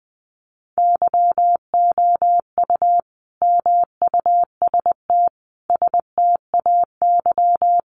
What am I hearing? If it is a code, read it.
Morse code, heard as a keyed tone, about 20 wpm, 700 Hz. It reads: YOU MUST STAY
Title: Sentences from Top 500 Words 20wpm